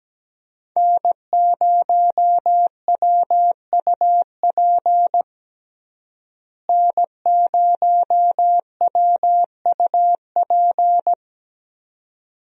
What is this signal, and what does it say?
Morse code, heard as a keyed tone, about 17 wpm, 700 Hz. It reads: N0WUP N0WUP
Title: USA Call Signs - 5 Characters - Repeated 2x - 17wpm